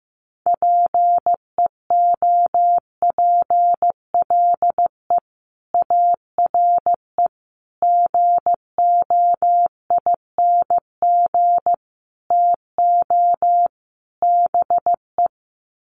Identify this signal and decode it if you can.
Morse code, heard as a keyed tone, about 15 wpm, 700 Hz. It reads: PEOPLE ARE GOING TO BE
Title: Sets of 5 Words Encore 15wpm